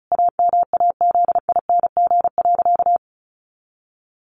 Morse code, heard as a keyed tone, about 34 wpm, 700 Hz. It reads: AMAZING.
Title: Sentences from Top 1500 Words 35wpm